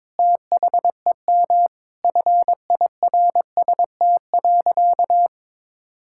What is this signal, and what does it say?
Morse code, heard as a keyed tone, about 22 wpm, 700 Hz. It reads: THEM FIRST.
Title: Sentences from Top 1000 Words - Review Part 6 - 22wpm